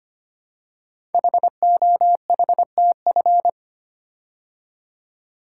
Morse code, heard as a keyed tone, about 25 wpm, 700 Hz. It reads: HOHTF
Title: Code Groups - 5 Letters - 25wpm